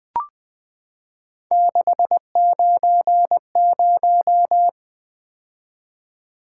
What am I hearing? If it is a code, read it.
Morse code, heard as a keyed tone, about 20 wpm, 700 Hz. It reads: 690
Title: Three Letter-Number Combinations - 20wpm